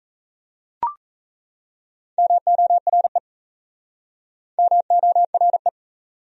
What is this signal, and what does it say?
Morse code, heard as a keyed tone, about 37 wpm, 700 Hz. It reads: MORE MORE